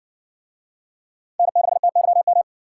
Morse code, heard as a keyed tone, about 60 wpm, 700 Hz. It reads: N6TYM